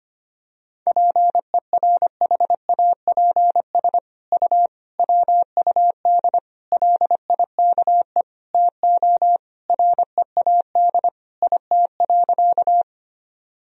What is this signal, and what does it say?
Morse code, heard as a keyed tone, about 25 wpm, 700 Hz. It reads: PERHAPS U WUD LIKE TO READ IT.
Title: Sentences from Top 500 Words and Abbreviations 25wpm